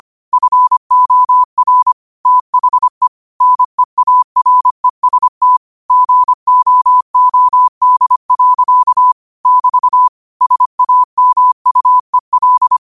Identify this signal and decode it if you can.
Morse code, heard as a keyed tone, about 25 wpm, 1000 Hz. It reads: FOR THE NEAREST GOOD. <BT> SAMUEL